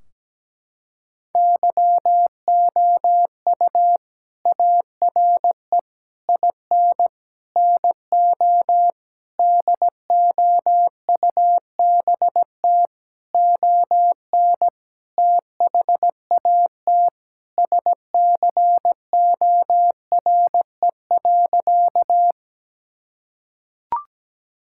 Morse code, heard as a keyed tone, about 17 wpm, 700 Hz. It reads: YOU ARE IN NO DOUBT ON THAT SCORE.
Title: Sentences from Top 1600 Words 17wpm